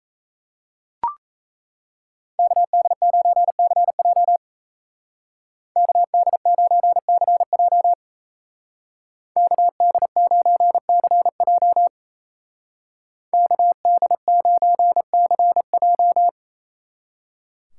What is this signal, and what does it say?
Morse code, heard as a keyed tone, about 35 wpm, 700 Hz. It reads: KD9CJ KD9CJ KD9CJ KD9CJ
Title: Speed Racing - USA Call Signs 28wpm